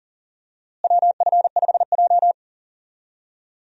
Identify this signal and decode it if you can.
Morse code, heard as a keyed tone, about 40 wpm, 700 Hz. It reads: WF5J
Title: USA Call Signs - 4 Characters - Repeated 2x - 40wpm